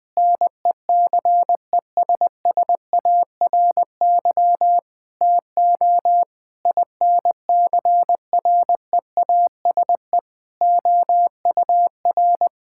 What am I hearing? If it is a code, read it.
Morse code, heard as a keyed tone, about 20 wpm, 700 Hz. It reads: NECESSARY TO INCREASE OUR
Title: Sentences from Top 1000 Words - Review Part 6 - 20wpm